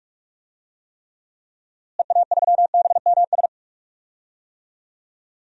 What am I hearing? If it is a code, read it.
Morse code, heard as a keyed tone, about 46 wpm, 700 Hz. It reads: EA3BMS
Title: DXCC-100 Call Signs - 6 Characters - Repeated 2x - 45wpm